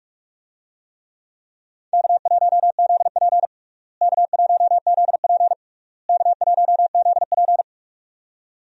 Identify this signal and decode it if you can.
Morse code, heard as a keyed tone, about 42 wpm, 700 Hz. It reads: K1ZP K1ZP K1ZP
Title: USA Call Signs - 4 Characters - Repeated 3x - 45wpm